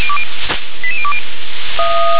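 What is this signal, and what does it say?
Five-tone selective call (ZVEI), heard as quick runs of five tones, tones of about 80 ms. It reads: 88020
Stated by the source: ZVEI five-tone selective call received on SDR.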